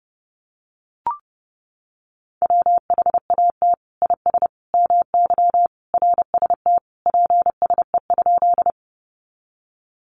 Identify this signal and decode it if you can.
Morse code, heard as a keyed tone, about 28 wpm, 700 Hz. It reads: WHAT IS MY RST PSE?
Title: Common QSO Phrases 30wpm